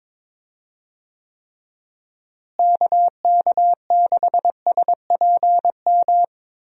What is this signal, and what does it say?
Morse code, heard as a keyed tone, about 23 wpm, 700 Hz. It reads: KK6SPM
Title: USA Call Signs - 6 Characters - Repeated 2x - 22wpm